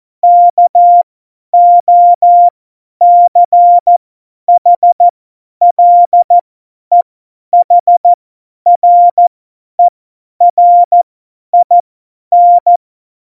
Morse code, heard as a keyed tone, about 14 wpm, spaced as for 10 wpm, 700 Hz. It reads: KOCHLEHRERIN
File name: Morsekurs / Witze / Witz 19. November 2019, 50 F70